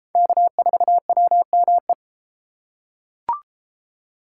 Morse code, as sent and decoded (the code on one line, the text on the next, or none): -.- ....- .-- -- .
K4WME